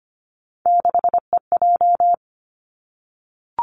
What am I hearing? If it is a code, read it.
Morse code, heard as a keyed tone, about 25 wpm, 700 Hz. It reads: 6EJ
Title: Three Challenging Letter-Number Combinations 25wpm